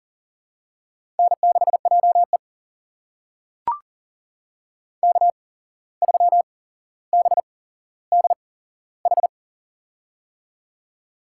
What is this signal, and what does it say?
Morse code, heard as a keyed tone, about 40 wpm, spaced as for 16 wpm, 700 Hz. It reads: N6JE K3BDH
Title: USA Call Signs - 8x Character Spacing - 40wpm